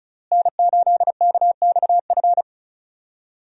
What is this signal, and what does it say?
Morse code, heard as a keyed tone, about 34 wpm, 700 Hz. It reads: N8KXF
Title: USA Call Signs - 5 Characters - Repeated 3x - 35wpm